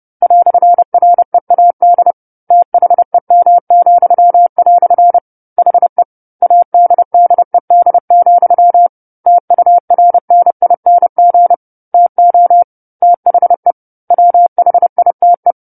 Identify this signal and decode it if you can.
Morse code, heard as a keyed tone, about 30 wpm, 700 Hz. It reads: "READ THEM," HE ADDED, TURNING TO THE WHITE
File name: 10-Chapter-10 30 WPM